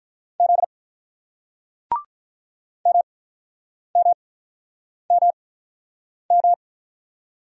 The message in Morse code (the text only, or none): Z MMMM